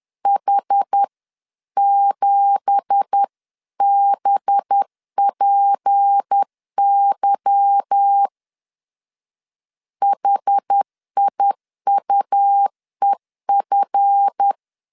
Morse code, heard as a keyed tone, about 10 wpm, 800 Hz. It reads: H7BPY HIUEF
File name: LEZIONI / 10 SIGNS / 10A SLASH 10wpm 2808